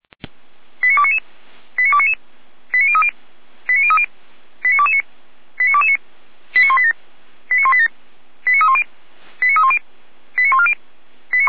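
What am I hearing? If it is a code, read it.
Five-tone selective call (ZVEI), heard as quick runs of five tones, tones of about 70 ms. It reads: 89299 89299 89030 89030 89229 89229 89187 89187 89310 89310 89140
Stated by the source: ZVEI five-tone call received by SDR, from feuerwehr-orlamuende.org (Feuerwehr Orlamünde) archive.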